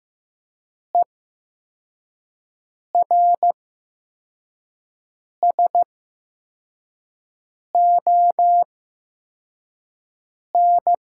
Morse code, heard as a keyed tone, about 15 wpm, spaced as for 4 wpm, 700 Hz. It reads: ERSON
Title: 8x Character Spacing - Top 500 Words - 15wpm